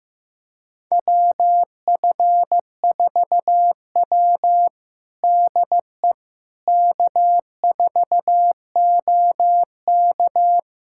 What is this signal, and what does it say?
Morse code, heard as a keyed tone, about 14 wpm, 700 Hz. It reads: WF4W DE K4OK
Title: USA Callsign DE Callsign - 4 Characters - 15wpm